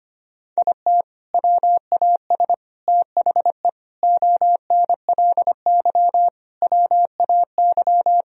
Morse code, heard as a keyed tone, about 25 wpm, 700 Hz. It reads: IT WAS THE ONLY WAY
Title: Sentences from Top 1000 Words - Review Part 3 - 25wpm